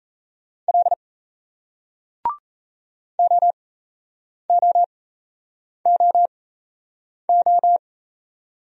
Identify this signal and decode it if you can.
Morse code, heard as a keyed tone, about 36 wpm, spaced as for 8 wpm, 700 Hz. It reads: L OOOO